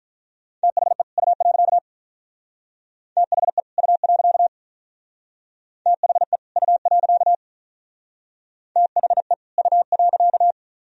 Morse code, heard as a keyed tone, about 42 wpm, 700 Hz. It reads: THE U. THE U. THE U. THE U.
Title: Speed-Racing - Sets of 2 Words 35wpm